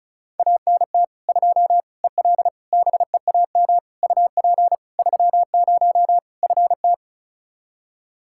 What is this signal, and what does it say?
Morse code, heard as a keyed tone, about 36 wpm, 700 Hz. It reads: ANT 2 EL BEAM UP 30 FT
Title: Practice QSO - Exchange 2 - Info - 35wpm